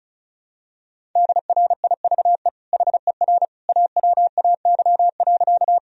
Morse code, heard as a keyed tone, about 36 wpm, 700 Hz. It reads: DRIVE HER AWAY.
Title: Sentences from Top 1000 Words - Review Part 2 - 35wpm